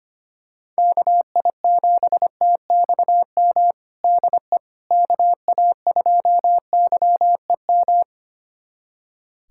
Morse code, heard as a keyed tone, about 25 wpm, 700 Hz. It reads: KI7TXM DE KA2YEM
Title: USA Callsign DE Callsign - 6 Characters - 25wpm